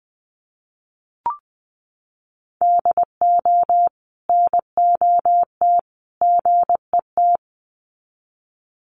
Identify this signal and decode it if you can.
Morse code, heard as a keyed tone, about 20 wpm, 700 Hz. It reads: DO NOT GET
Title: Sets of 3 Words 20wpm